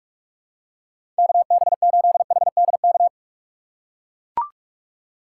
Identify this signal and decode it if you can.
Morse code, heard as a keyed tone, about 42 wpm, 700 Hz. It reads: KB8HDK
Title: USA Call Signs - 6 Characters - 4x Character Spacing - 45wpm